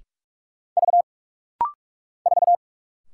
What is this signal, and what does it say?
Morse code, heard as a keyed tone, about 44 wpm, 700 Hz. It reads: V 4
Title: Single Letter-Number - Rapid-Fire 45wpm